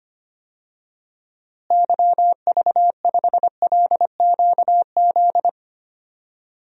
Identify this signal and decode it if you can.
Morse code, heard as a keyed tone, about 25 wpm, 700 Hz. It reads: YV5LQZ
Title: DXCC-100 Call Signs - 6 Characters - Repeated 2x - 25wpm